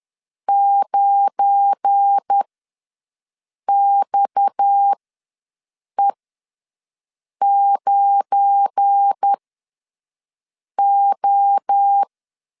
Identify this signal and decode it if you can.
Morse code, heard as a keyed tone, about 10 wpm, spaced as for 5 wpm, 800 Hz. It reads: 9XE9O
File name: LEZIONI / 7 GF9Z / 7 GF9 ws 10wpm 4034